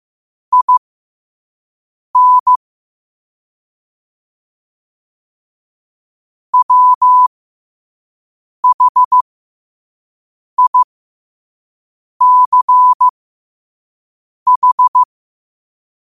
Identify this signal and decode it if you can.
Morse code, heard as a keyed tone, about 15 wpm, spaced as for 5 wpm, 1000 Hz. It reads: IN WHICH
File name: qotdfiles / 5wpm